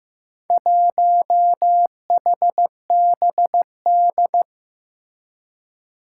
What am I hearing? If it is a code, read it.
Morse code, heard as a keyed tone, about 15 wpm, 700 Hz. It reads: 1HBD
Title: Four Challenging Letter-Number Combinations 15wpm